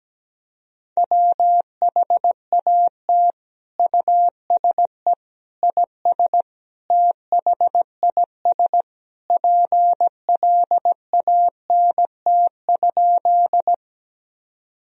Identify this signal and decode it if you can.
Morse code, heard as a keyed tone, about 17 wpm, 700 Hz. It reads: WHAT USE IS THIS PLANT?
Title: Sentences from Top 1000 Words - Review Part 1 - 17wpm